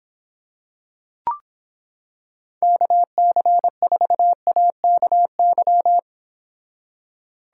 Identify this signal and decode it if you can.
Morse code, heard as a keyed tone, about 26 wpm, 700 Hz. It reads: KC4AKY